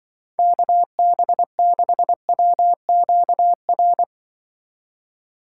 Morse code, heard as a keyed tone, about 24 wpm, 700 Hz. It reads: KB6WQR
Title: Speed Racing - USA Call Signs 20wpm